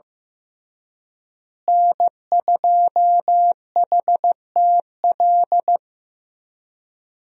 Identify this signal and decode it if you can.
Morse code, heard as a keyed tone, about 15 wpm, 700 Hz. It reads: N2HTL
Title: USA Call Signs - 5 Characters - Repeated 3x - 15wpm